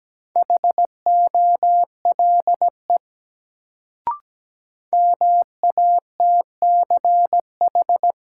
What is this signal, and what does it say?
Morse code, heard as a keyed tone, about 17 wpm, 700 Hz. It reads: HOLE MATCH
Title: Sentences from Top 1600 Words - New Word Drill - 17wpm